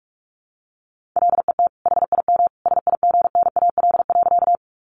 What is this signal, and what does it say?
Morse code, heard as a keyed tone, about 44 wpm, 700 Hz. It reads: LET HIM SIGNAL.
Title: Sentences from Top 1400 Words 45wpm